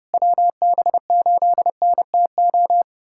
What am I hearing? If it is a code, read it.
Morse code, heard as a keyed tone, about 30 wpm, 700 Hz. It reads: WB8NTO
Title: USA Callsign DE Callsign 30wpm